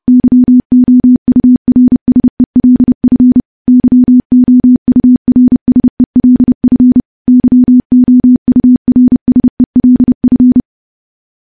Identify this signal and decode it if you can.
Morse code, heard as a keyed tone, about 28 wpm, 250 Hz. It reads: YOURSELF YOURSELF YOURSELF
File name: morse / morse-training-30WPM